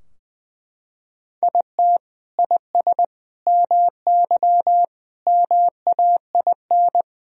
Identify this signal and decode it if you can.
Morse code, heard as a keyed tone, about 21 wpm, 700 Hz. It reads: IT IS MY MAIN
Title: Sentences from Top 1000 Words - Review Part 2 - 20wpm